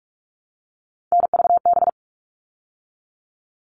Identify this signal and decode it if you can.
Morse code, heard as a keyed tone, about 44 wpm, 700 Hz. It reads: NVB
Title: All Cut Numbers - 3-Digit 45wpm